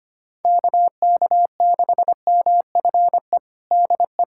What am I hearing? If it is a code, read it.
Morse code, heard as a keyed tone, about 25 wpm, 700 Hz. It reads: KK6MFE DE
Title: USA Callsign DE Callsign - 6 Characters - 25wpm